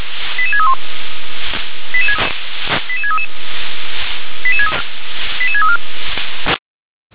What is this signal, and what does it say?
Five-tone selective call (ZVEI), heard as quick runs of five tones, tones of about 70 ms. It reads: 88631 88633 88634